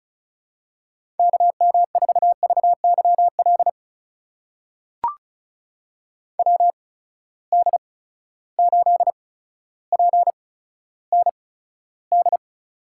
Morse code, heard as a keyed tone, about 36 wpm, spaced as for 15 wpm, 700 Hz. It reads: KM4VYL WD8PND
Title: USA Call Signs - 6 Characters - 8x Character Spacing - 35wpm